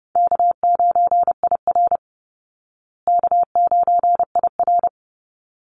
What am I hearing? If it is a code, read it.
Morse code, heard as a keyed tone, about 30 wpm, 700 Hz. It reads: K9IR K9IR
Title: Easy Call Signs Repeated 2x - 30wpm